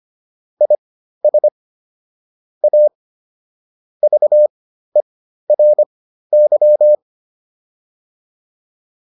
Morse code, heard as a keyed tone, about 25 wpm, spaced as for 13 wpm, 600 Hz. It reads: IS A VERY